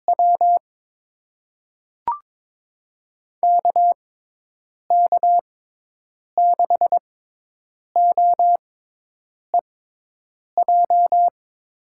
Morse code, heard as a keyed tone, about 21 wpm, spaced as for 8 wpm, 700 Hz. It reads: W KK6OEJ
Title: USA Call Signs - 6 Characters - 6x Character Spacing - 22wpm